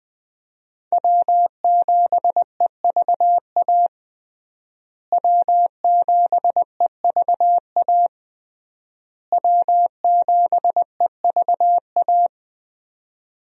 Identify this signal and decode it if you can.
Morse code, heard as a keyed tone, about 20 wpm, 700 Hz. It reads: W7EVA W7EVA W7EVA
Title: USA Call Signs - 5 Characters - Repeated 3x - 20wpm